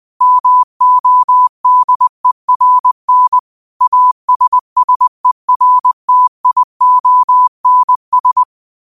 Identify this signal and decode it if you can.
Morse code, heard as a keyed tone, about 20 wpm, 1000 Hz. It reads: MODERN ASSERTIONS
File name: qotdfiles / 20wpm